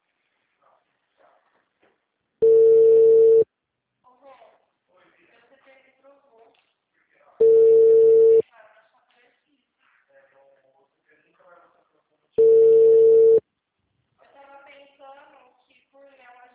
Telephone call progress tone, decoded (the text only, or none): ringback tone